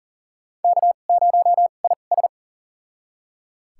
Morse code, heard as a keyed tone, about 40 wpm, spaced as for 29 wpm, 700 Hz. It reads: K0IS
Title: USA Call Signs - 4 Characters - 2x Character Spacing - 40wpm